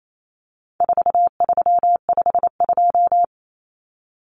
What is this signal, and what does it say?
Morse code, heard as a keyed tone, about 28 wpm, 700 Hz. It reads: 4352
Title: Common Cut Numbers - 4-Digit 28wpm